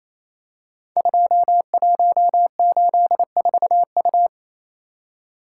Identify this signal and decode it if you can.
Morse code, heard as a keyed tone, about 29 wpm, 700 Hz. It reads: 2184U
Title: Code Groups - 5 Letter:Numbers - 28wpm